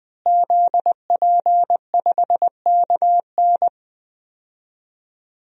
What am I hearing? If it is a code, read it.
Morse code, heard as a keyed tone, about 21 wpm, 700 Hz. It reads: ZP5KN
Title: DXCC-100 Call Signs - 5 Characters - Repeated 2x - 20wpm